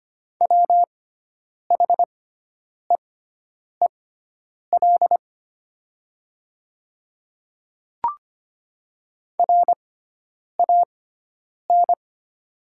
Morse code, heard as a keyed tone, about 25 wpm, spaced as for 9 wpm, 700 Hz. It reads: WHEEL RAN
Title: 6x Character Spacing - Top 500 Words - 25wpm